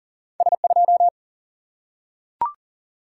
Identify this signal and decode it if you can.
Morse code, heard as a keyed tone, about 39 wpm, 700 Hz. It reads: S2
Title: Single-Letter Single-Number 40wpm